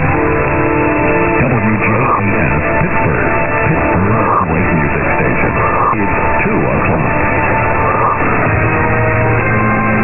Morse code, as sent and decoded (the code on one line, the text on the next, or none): none